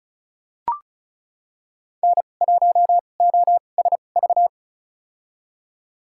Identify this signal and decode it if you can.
Morse code, heard as a keyed tone, about 34 wpm, spaced as for 25 wpm, 690 Hz. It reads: N1OSV
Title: USA Call Signs - 5 Characters - 2x Character Spacing - 35wpm